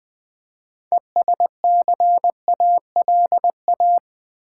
Morse code, heard as a keyed tone, about 20 wpm, 700 Hz. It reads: ESCALA
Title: Build A Word - 3 Letters At A Time - 8-Letter Words - 20wpm